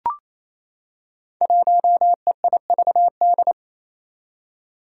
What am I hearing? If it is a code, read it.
Morse code, heard as a keyed tone, about 28 wpm, 700 Hz. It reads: 1EIVD
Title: Five Challenging Letter-Number Combinations 28wpm